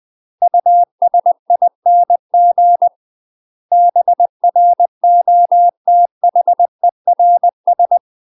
Morse code, heard as a keyed tone, about 20 wpm, 700 Hz. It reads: USING BROTHERS
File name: UK MASTER QSOs TRAINING / 20WPM / UK MASTER QSOs 0052